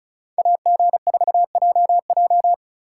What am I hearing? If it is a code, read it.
Morse code, heard as a keyed tone, about 44 wpm, 700 Hz. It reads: AG4JJ